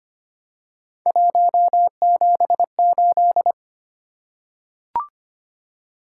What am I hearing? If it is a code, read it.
Morse code, heard as a keyed tone, about 26 wpm, 700 Hz. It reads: 178
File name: Three Letter-Number Combinations - 25wpm